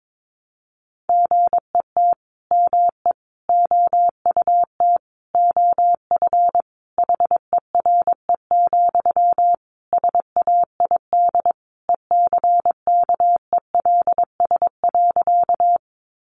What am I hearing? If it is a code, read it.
Morse code, heard as a keyed tone, about 22 wpm, 700 Hz. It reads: GET ME OUT OF HERE, SAID ECKELS.
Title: A Sound Of Thunder 22wpm